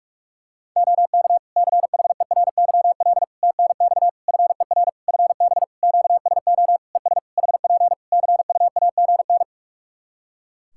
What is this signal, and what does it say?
Morse code, heard as a keyed tone, about 46 wpm, 700 Hz. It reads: OK CHERYL TNX FER FB QSO ES HP CUAGN
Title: Practice QSO - Exchange 4 - Ending - 45wpm